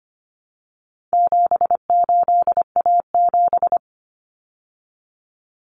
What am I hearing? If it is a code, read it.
Morse code, heard as a keyed tone, about 25 wpm, 700 Hz. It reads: 78A7
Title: Common Cut Numbers - 4-Digit 25wpm